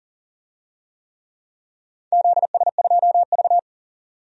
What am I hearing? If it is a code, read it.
Morse code, heard as a keyed tone, about 44 wpm, 690 Hz. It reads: ZS2V
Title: DXCC-100 Call Signs - 4 Characters - 40wpm